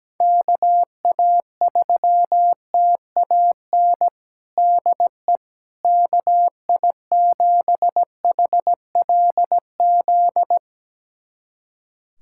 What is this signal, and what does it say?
Morse code, heard as a keyed tone, about 17 wpm, 700 Hz. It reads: KA3TAN DE KI7HLZ
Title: USA Callsign DE Callsign - 6 Characters - 17wpm